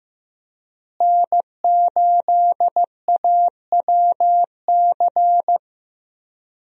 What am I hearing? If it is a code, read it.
Morse code, heard as a keyed tone, about 15 wpm, 700 Hz. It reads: N8AWC
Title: USA Call Signs - 5 Characters - Repeated 2x - 15wpm